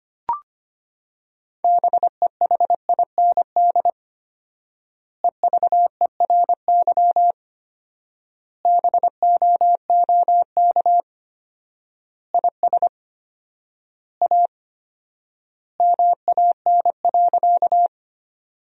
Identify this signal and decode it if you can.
Morse code, heard as a keyed tone, about 25 wpm, 700 Hz. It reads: BEHIND EVERY BOOK IS A MAN.
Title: 4x Word-Spacing - Sentences from Top 500 Words - 25wpm